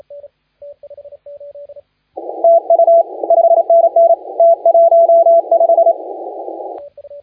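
Morse code, heard as a keyed tone, about 28 wpm, 670 Hz, with radio noise. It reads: TU 5NN T15
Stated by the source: Morse contest audio recorded by LA4C during the SAC CW 2018 contest.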